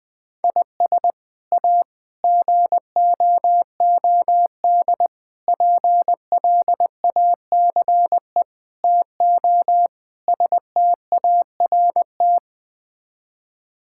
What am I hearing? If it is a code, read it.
Morse code, heard as a keyed tone, about 20 wpm, 700 Hz. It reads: IS A GOOD PLACE TO START
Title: Sets of 6 Words - 20wpm